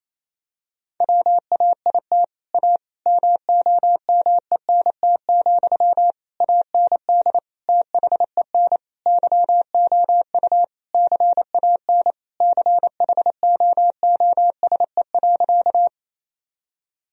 Morse code, heard as a keyed tone, about 28 wpm, 700 Hz. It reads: WAIT A MOMENT, AND THEN YOU CAN CHOOSE.
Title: Sentences from Top 1000 Words - Review Part 4 - 28wpm